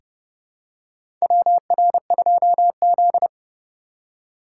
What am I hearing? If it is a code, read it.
Morse code, heard as a keyed tone, about 30 wpm, 700 Hz. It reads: WR2Z
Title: USA Call Signs - 4 Characters - 4x Character Spacing - 30wpm